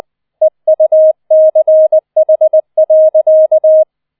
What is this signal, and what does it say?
Morse code, heard as a keyed tone, about 20 wpm, 630 Hz. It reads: EUCH.